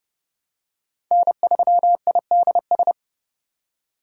Morse code, heard as a keyed tone, about 29 wpm, 700 Hz. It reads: N3IDS
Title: USA Call Signs - 3x Character Spacing - 30wpm